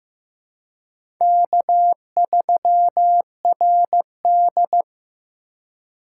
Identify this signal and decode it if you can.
Morse code, heard as a keyed tone, about 15 wpm, 700 Hz. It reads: K3RD
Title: USA Call Signs - 4 Characters - 4x Character Spacing - 15wpm